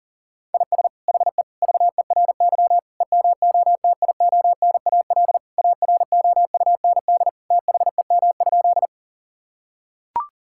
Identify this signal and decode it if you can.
Morse code, heard as a keyed tone, about 40 wpm, 700 Hz. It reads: IS HE VERY EMOTIONAL AROUND THEM?
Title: Sentences from Top 1000 Words - Review Part 5 - 40wpm